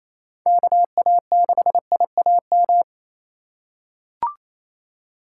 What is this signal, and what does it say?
Morse code, heard as a keyed tone, about 28 wpm, 690 Hz. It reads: KA6IAM
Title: USA Call Signs - 6 Characters - 2x Character Spacing - 28wpm